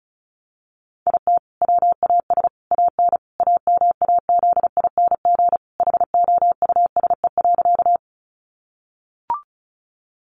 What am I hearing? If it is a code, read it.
Morse code, heard as a keyed tone, about 34 wpm, 700 Hz. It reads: IT WAS AN AMAZING HOUSE.
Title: Sentences from Top 1500 Words 35wpm